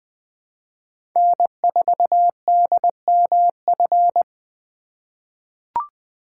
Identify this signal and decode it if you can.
Morse code, heard as a keyed tone, about 21 wpm, 700 Hz. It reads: N4DMF
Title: USA Call Signs - 3x Character Spacing - 20wpm